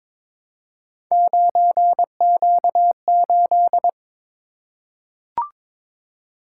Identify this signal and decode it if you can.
Morse code, heard as a keyed tone, about 22 wpm, 700 Hz. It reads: 9Q8
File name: practice / 22wpm / Three Letter-Number Combinations - 22wpm